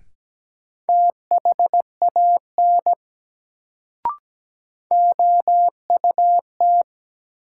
Morse code, heard as a keyed tone, about 17 wpm, 700 Hz. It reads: THAN OUT
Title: Sets of 2 Words - New Word Drill - 17wpm